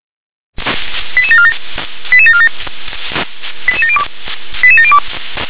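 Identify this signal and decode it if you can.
Five-tone selective call (ZVEI), heard as quick runs of five tones, tones of about 80 ms. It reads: 88747 88747 88702 88702
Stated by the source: ZVEI five-tone selective call received on SDR.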